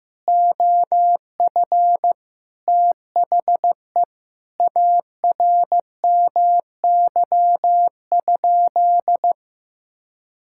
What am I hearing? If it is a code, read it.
Morse code, heard as a keyed tone, about 16 wpm, 700 Hz. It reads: OF THE ARMY?
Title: Sentences from Top 1000 Words - Review Part 6 - 15wpm